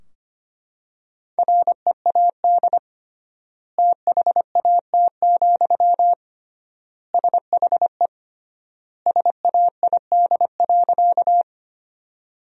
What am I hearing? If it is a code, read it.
Morse code, heard as a keyed tone, about 25 wpm, 700 Hz. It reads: READ THAT, SHE SAID.
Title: 3x Word-Spacing - Sentences from Top 200 Words - 25wpm